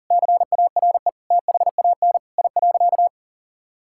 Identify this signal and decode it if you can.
Morse code, heard as a keyed tone, about 40 wpm, 700 Hz. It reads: CARE THAN I.
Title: Sentences from Top 1000 Words - Review Part 5 - 40wpm